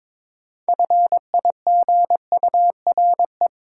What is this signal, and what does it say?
Morse code, heard as a keyed tone, about 22 wpm, 700 Hz. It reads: FIGURE
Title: Sentences from Top 1400 Words - New Word Drill Alt - 22wpm